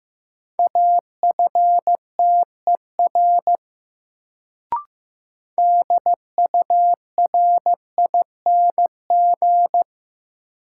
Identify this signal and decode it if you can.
Morse code, heard as a keyed tone, about 15 wpm, 700 Hz. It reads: AFTER DURING
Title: Sets of 2 Words - New Word Drill - 15wpm